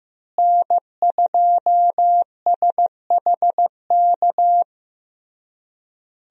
Morse code, heard as a keyed tone, about 15 wpm, 700 Hz. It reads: N2SHK
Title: USA Call Signs - 5 Characters - Repeated 3x - 15wpm